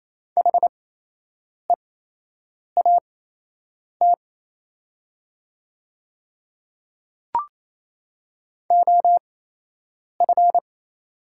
Morse code, heard as a keyed tone, about 28 wpm, spaced as for 8 wpm, 700 Hz. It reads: HEAT OF
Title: 8x Character Spacing - Top 500 Words - 28wpm